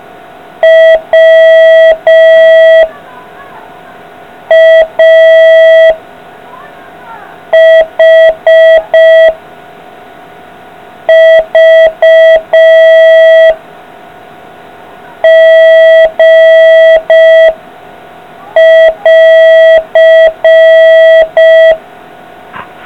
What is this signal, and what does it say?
Morse code, heard as a keyed tone, about 5 wpm, 650 Hz. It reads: W A H V G <AR>